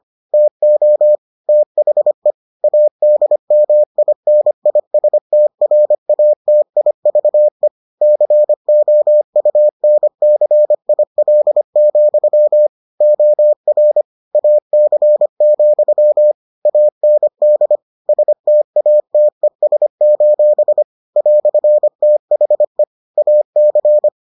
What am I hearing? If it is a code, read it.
Morse code, heard as a keyed tone, about 25 wpm, 600 Hz. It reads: TO THE ADMINISTRATIVE COUNCIL, OR AC, AND STATES: "THE AC